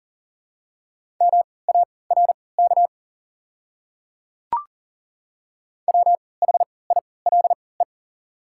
Morse code, heard as a keyed tone, about 40 wpm, spaced as for 23 wpm, 700 Hz. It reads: MARK WHILE
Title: 3x Character Spacing - Top 300 Words - 40wpm